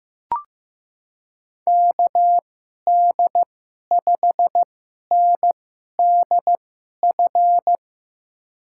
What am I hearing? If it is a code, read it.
Morse code, heard as a keyed tone, about 15 wpm, spaced as for 11 wpm, 700 Hz. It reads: KD5NDF
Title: USA Call Signs - 6 Characters - 2x Character Spacing - 15wpm